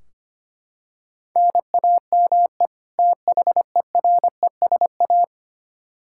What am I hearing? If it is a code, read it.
Morse code, heard as a keyed tone, about 25 wpm, 700 Hz. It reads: NAME THERESA
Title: K1USN SST QSO Practice 25wpm